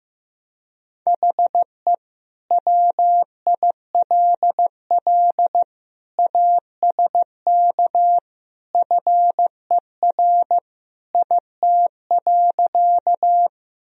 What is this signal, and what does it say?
Morse code, heard as a keyed tone, about 16 wpm, 700 Hz. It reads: HE WILL ASK FER IT.
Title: Sentences from Top 200 Words and Abbreviations 15wpm